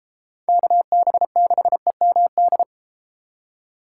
Morse code, heard as a keyed tone, about 31 wpm, 700 Hz. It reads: KB6EMD